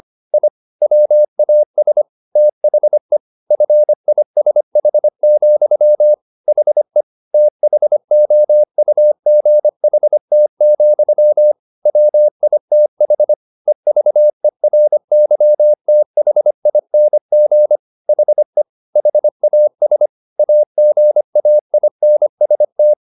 Morse code, heard as a keyed tone, about 25 wpm, 600 Hz. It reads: I WAS THE FISH, HE THOUGHT, WITH EVERYTHING HE HAS AGAINST
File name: the old man and the sea part 1 25wpm